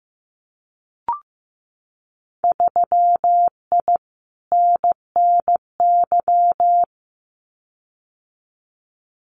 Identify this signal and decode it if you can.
Morse code, heard as a keyed tone, about 15 wpm, 700 Hz. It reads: 3I NNY
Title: Winter Field Day Exchanges 15wpm